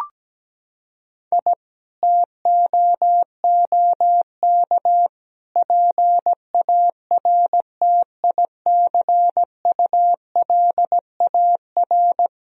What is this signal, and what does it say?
Morse code, heard as a keyed tone, about 17 wpm, 700 Hz. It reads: I TOOK PARTICULAR
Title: Sentences from Top 1000 Words - Review Part 4 - 17wpm